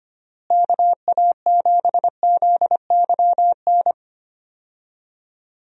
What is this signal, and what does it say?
Morse code, heard as a keyed tone, about 25 wpm, 700 Hz. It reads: KA7ZYN
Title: USA Call Signs - 6 Characters - Repeated 2x - 25wpm